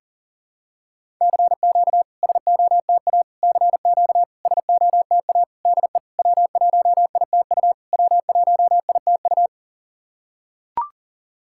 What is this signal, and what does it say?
Morse code, heard as a keyed tone, about 40 wpm, 690 Hz. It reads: CQ SOTA CQ SOTA DE W1ITU W1ITU
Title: SOTA QSO Practice 40wpm